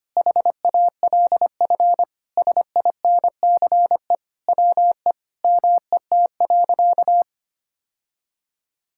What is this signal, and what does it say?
Morse code, heard as a keyed tone, about 25 wpm, 700 Hz. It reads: HALF SINCE WE MET.
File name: practice / 25wpm / Sentences with 4-Word Phrases - 25wpm